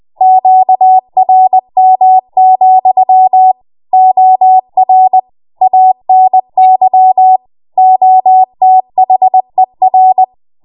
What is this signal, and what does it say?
Morse code, heard as a keyed tone, about 20 wpm, 750 Hz. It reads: QRM, OR ANY OTHER
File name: Morse / 20WPM